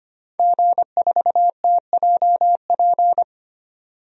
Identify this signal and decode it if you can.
Morse code, heard as a keyed tone, about 25 wpm, 690 Hz. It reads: G4TJP
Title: DXCC-100 Call Signs - 5 Characters - 25wpm